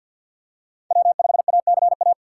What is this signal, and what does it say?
Morse code, heard as a keyed tone, about 54 wpm, 700 Hz. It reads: W5ACA
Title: USA Call Signs - 5 Characters - 6x Character Spacing - 50wpm